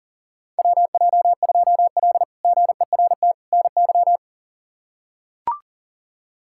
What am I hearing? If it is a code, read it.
Morse code, heard as a keyed tone, about 40 wpm, 700 Hz. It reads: WJ2L GERT NY
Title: K1USN SST QSO Practice - 40wpm